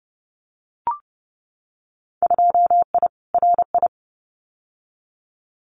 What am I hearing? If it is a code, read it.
Morse code, heard as a keyed tone, about 30 wpm, 700 Hz. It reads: 2I RI
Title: Winter Field Day Exchanges Encore 30wpm